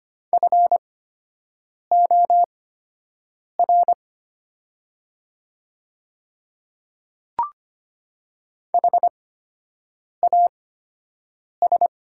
Morse code, heard as a keyed tone, about 25 wpm, spaced as for 7 wpm, 700 Hz. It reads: FOR HAS
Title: 8x Character Spacing - Top 100 Words - 25wpm